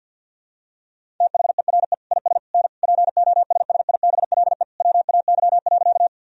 Morse code, heard as a keyed tone, about 50 wpm, 700 Hz. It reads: THERE IS N POSSIBLE WAY.
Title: Sentences from Top 500 Words and Abbreviations 50wpm